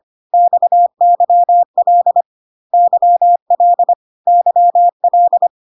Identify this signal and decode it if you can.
Morse code, heard as a keyed tone, about 25 wpm, 700 Hz. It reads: XYL YL YL
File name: cw / CW common abbreviations alphabetical repeated 3 times 25 25wpm ik1hge